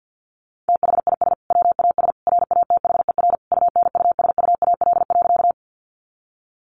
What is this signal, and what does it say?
Morse code, heard as a keyed tone, about 50 wpm, 700 Hz. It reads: THIS WAS RATHER UNUSUAL.
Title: Sentences from Top 1400 Words 50wpm